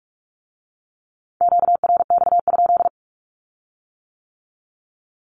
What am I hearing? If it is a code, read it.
Morse code, heard as a keyed tone, about 46 wpm, 700 Hz. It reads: QRX?
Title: Q-Codes with Spoken Meaning - 45wpm